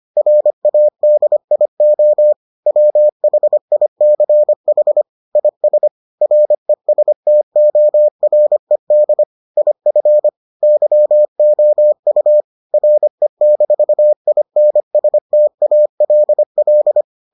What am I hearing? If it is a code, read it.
Morse code, heard as a keyed tone, about 25 wpm, 600 Hz. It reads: RADIO WHICH IS RESTORED IF YOU RE-INSTALL